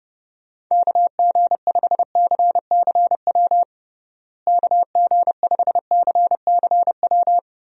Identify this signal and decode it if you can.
Morse code, heard as a keyed tone, about 30 wpm, 700 Hz. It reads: KG5CCW KG5CCW
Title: USA Call Signs - 6 Characters - Repeated 3x - 30wpm